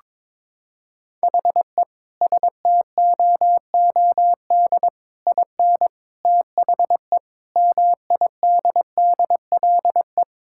Morse code, heard as a keyed tone, about 22 wpm, 700 Hz. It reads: HE STOOD IN THE MIDDLE
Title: Sentences from Top 1000 Words - Review Part 6 - 22wpm